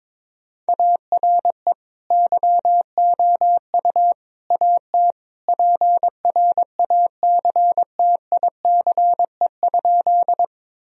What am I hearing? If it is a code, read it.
Morse code, heard as a keyed tone, about 21 wpm, 700 Hz. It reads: ARE YOU AT PRACTICE?
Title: Sentences from Top 1000 Words - Review Part 4 - 22wpm